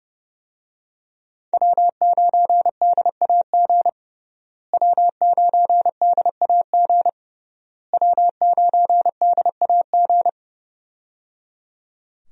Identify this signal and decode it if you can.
Morse code, heard as a keyed tone, about 30 wpm, 700 Hz. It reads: W9DAG W9DAG W9DAG
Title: USA Call Signs - 5 Characters - Repeated 3x - 30wpm